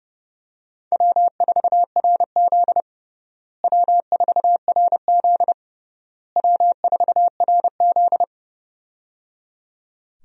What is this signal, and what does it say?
Morse code, heard as a keyed tone, about 30 wpm, 700 Hz. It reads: W4RZ W4RZ W4RZ
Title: USA Call Signs - 4 Characters - Repeated 3x - 30wpm